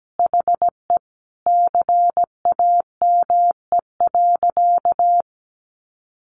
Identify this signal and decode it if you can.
Morse code, heard as a keyed tone, about 17 wpm, 700 Hz. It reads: HE CAME.
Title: Sentences from Top 1500 Words 17wpm